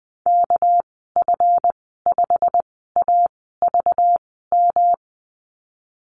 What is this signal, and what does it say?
Morse code, heard as a keyed tone, about 20 wpm, spaced as for 15 wpm, 700 Hz. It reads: KF5AVM
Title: 207 Lesson 33g - Letter j - Call Signs 20wpm